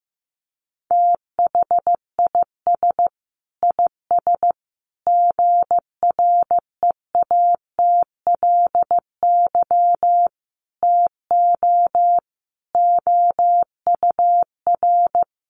Morse code, heard as a keyed tone, about 15 wpm, 700 Hz. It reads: THIS IS GREATLY TO OUR
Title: Sentences from Top 1400 Words 15wpm